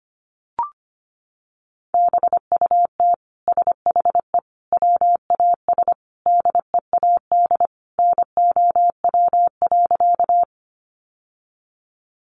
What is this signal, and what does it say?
Morse code, heard as a keyed tone, about 25 wpm, 700 Hz. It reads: BUT SHE WAS DEAD NOW.
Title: Sentences from Top 1500 Words 25wpm